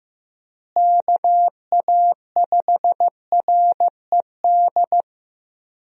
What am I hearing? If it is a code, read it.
Morse code, heard as a keyed tone, about 16 wpm, 700 Hz. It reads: KA5RED
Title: USA Call Signs - 6 Characters - Repeated 3x - 15wpm